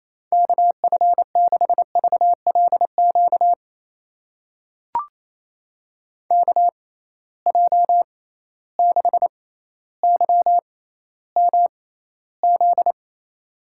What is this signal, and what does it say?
Morse code, heard as a keyed tone, about 28 wpm, spaced as for 14 wpm, 700 Hz. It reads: KF6VLQ KJ6YMZ
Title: USA Call Signs - 6 Characters - 6x Character Spacing - 28wpm